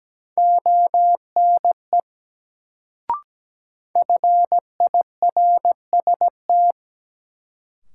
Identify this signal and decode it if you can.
Morse code, heard as a keyed tone, about 17 wpm, 700 Hz. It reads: ONE FIRST
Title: Sets of 2 Words - New Word Drill - 17wpm